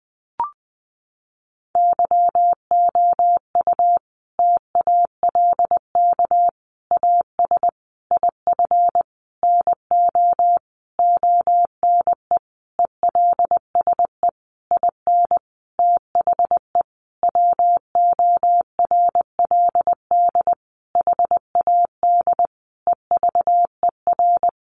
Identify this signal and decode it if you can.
Morse code, heard as a keyed tone, about 20 wpm, 700 Hz. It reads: YOU TALK AS IF NO ONE ELSE IN THE WORLD HAD EVER
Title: Sentences from Top 1100 Words 20wpm